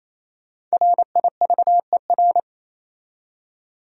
Morse code, heard as a keyed tone, about 36 wpm, 700 Hz. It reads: RIVER